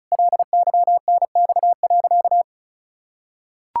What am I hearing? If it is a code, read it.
Morse code, heard as a keyed tone, about 36 wpm, 700 Hz. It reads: LYNX.
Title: Pangram Challenge v2 - 35wpm